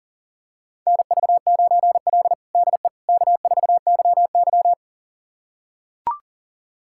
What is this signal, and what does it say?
Morse code, heard as a keyed tone, about 37 wpm, 700 Hz. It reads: NV9L DE K4YY
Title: USA Callsign DE Callsign - 4 Characters - 40wpm